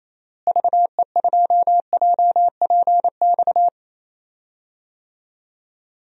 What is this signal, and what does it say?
Morse code, heard as a keyed tone, about 28 wpm, 700 Hz. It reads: VE2JPX
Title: DXCC-100 Call Signs - 6 Characters - Repeated 2x - 28wpm